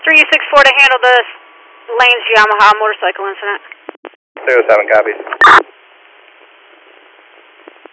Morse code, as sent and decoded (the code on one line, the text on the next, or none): none